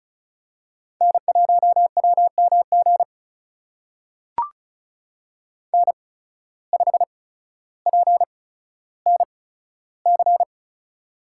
Morse code, heard as a keyed tone, about 36 wpm, spaced as for 15 wpm, 700 Hz. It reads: N1WMG N5PNC
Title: USA Call Signs - 5 Characters - 8x Character Spacing - 35wpm